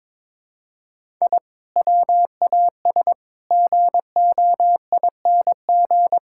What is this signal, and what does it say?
Morse code, heard as a keyed tone, about 22 wpm, 700 Hz. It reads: I WAS GOING
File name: practice / 22wpm / Sentences with 4-Word Phrases - 22wpm